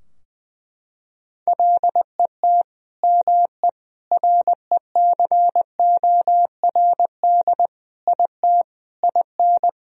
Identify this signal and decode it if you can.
Morse code, heard as a keyed tone, about 21 wpm, 700 Hz. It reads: LET ME RECORD IT IN
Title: Sentences from Top 1000 Words - Review Part 3 - 20wpm